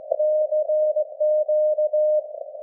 Morse code, heard as a keyed tone, about 16 wpm, 610 Hz. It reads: CQ
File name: laengstwellenempfang / SDRSharp 17kHz AF